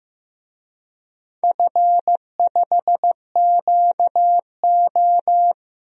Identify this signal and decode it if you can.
Morse code, heard as a keyed tone, about 15 wpm, 700 Hz. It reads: F5QO
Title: DXCC-100 Call Signs - 4 Characters - 15wpm